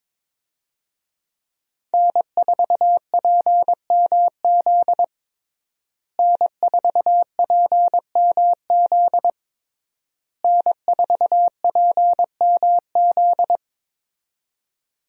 Morse code, heard as a keyed tone, about 22 wpm, 700 Hz. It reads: N4PMZ N4PMZ N4PMZ
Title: USA Call Signs - 5 Characters - Repeated 3x - 22wpm